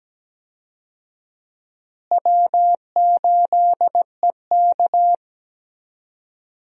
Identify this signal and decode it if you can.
Morse code, heard as a keyed tone, about 17 wpm, 700 Hz. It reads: W8EK
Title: USA Call Signs - 4 Characters - Repeated 2x - 17wpm